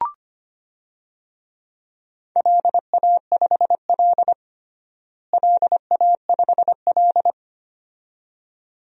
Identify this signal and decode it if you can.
Morse code, heard as a keyed tone, about 25 wpm, 700 Hz. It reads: LA5L LA5L
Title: DXCC-100 Call Signs - 4 Characters - Repeated 2x - 25wpm